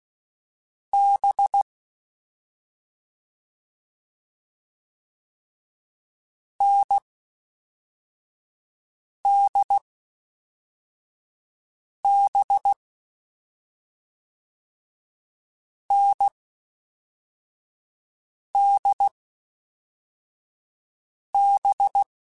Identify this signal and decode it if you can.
Morse code, heard as a keyed tone, about 16 wpm, 800 Hz. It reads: B N D B N D B
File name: Difficult Combos / Difficult Combos-mp3s / NDB NDB6 @5